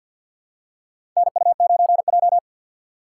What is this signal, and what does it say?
Morse code, heard as a keyed tone, about 52 wpm, 700 Hz. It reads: NU9J
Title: USA Call Signs - 2x Character Spacing - 50wpm